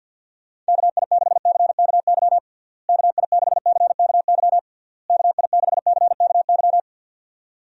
Morse code, heard as a keyed tone, about 50 wpm, 700 Hz. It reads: KI6CKY KI6CKY KI6CKY
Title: USA Call Signs - 6 Characters - Repeated 3x - 50wpm